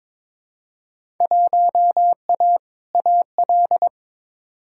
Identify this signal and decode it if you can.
Morse code, heard as a keyed tone, about 22 wpm, 700 Hz. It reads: 1A AL
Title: Field Day Exchanges - 22wpm